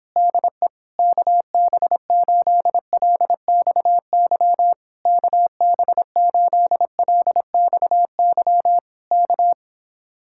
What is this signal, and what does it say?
Morse code, heard as a keyed tone, about 26 wpm, 700 Hz. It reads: DE KB8LXY KB8LXY K